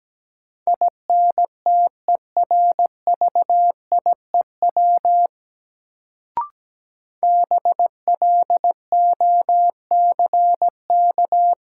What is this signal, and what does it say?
Morse code, heard as a keyed tone, about 17 wpm, 700 Hz. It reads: INTERVIEW BLOCK
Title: Sentences from Top 1600 Words - New Word Drill - 17wpm